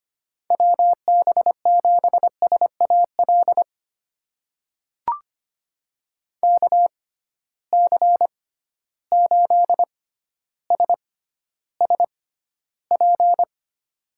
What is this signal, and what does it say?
Morse code, heard as a keyed tone, about 25 wpm, spaced as for 13 wpm, 700 Hz. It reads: WB7SAL KC8SSP
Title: USA Call Signs - 6 Characters - 6x Character Spacing - 25wpm